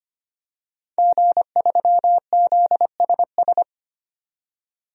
Morse code, heard as a keyed tone, about 25 wpm, 700 Hz. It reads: G3ZSS
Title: DXCC-100 Call Signs - 5 Characters - 25wpm